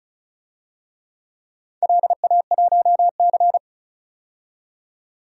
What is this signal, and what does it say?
Morse code, heard as a keyed tone, about 36 wpm, 700 Hz. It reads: LA1C
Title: DXCC-100 Call Signs - 4 Characters - 35wpm